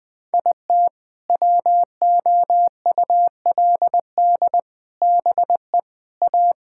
Morse code, heard as a keyed tone, about 20 wpm, 700 Hz. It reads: IT WOULD BE A
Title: Sentences with 7-Word Phrases - 20wpm